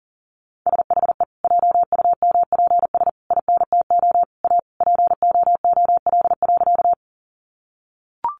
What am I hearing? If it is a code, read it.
Morse code, heard as a keyed tone, about 38 wpm, 700 Hz. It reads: SHE JUMPS INTO A POOL.
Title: Additional Easy Phrases 40wpm